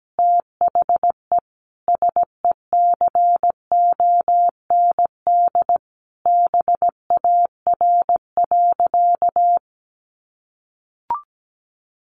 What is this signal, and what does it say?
Morse code, heard as a keyed tone, about 17 wpm, 700 Hz. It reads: THE SECOND BAR.
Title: Easy Phrases 17wpm